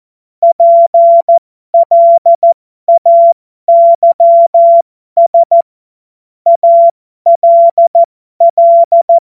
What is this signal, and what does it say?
Morse code, heard as a keyed tone, about 14 wpm, spaced as for 12 wpm, 680 Hz. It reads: PLAYS ALL
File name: Morsekurs / Witze